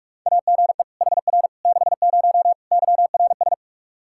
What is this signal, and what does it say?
Morse code, heard as a keyed tone, about 46 wpm, 700 Hz. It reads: AGE HR 60 YRS
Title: Practice QSO - Exchange 3 - More Info - 45wpm